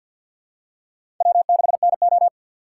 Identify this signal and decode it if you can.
Morse code, heard as a keyed tone, about 52 wpm, 700 Hz. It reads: W6NO
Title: USA Call Signs - 4 Characters - 2x Character Spacing - 50wpm